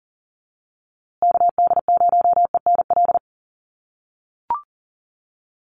Morse code, heard as a keyed tone, about 40 wpm, 700 Hz. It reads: KD0ENL
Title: USA Call Signs Repeated 3x 40wpm